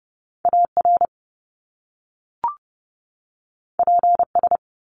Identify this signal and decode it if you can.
Morse code, heard as a keyed tone, about 30 wpm, 700 Hz. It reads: AR PS
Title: Top Two Letter Combinations in Words 30wpm